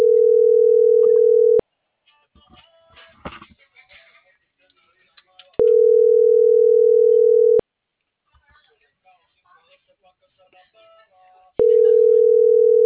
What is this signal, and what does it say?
Ringback tone (North America), 440 + 480 Hz.
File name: CENTRAL SAGOT / out edgar.sagot - .4378